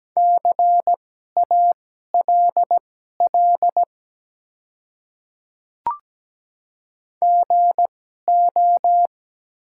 Morse code, heard as a keyed tone, about 17 wpm, spaced as for 12 wpm, 700 Hz. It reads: CALL GO
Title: 2x Character Spacing - Top 100 Words - 17wpm